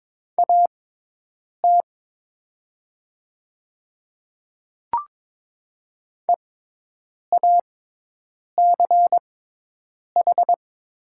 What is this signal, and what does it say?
Morse code, heard as a keyed tone, about 22 wpm, spaced as for 8 wpm, 700 Hz. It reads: AT EACH
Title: 6x Character Spacing - Top 200 Words - 22wpm